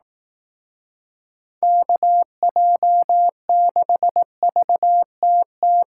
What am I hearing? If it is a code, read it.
Morse code, heard as a keyed tone, about 18 wpm, 700 Hz. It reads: KJ6VTT
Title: Speed Racing - USA 2x3 Call Signs 15wpm